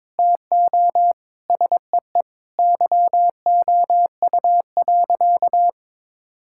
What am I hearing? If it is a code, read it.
Morse code, heard as a keyed tone, about 22 wpm, 700 Hz. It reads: TO SEE YOU.
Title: Sentences from Top 1000 Words - Review Part 5 - 22wpm